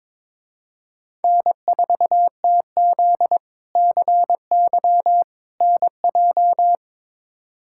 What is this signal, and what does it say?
Morse code, heard as a keyed tone, about 22 wpm, 700 Hz. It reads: N4TZ CY NJ
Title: K1USN SST QSO Practice - 22wpm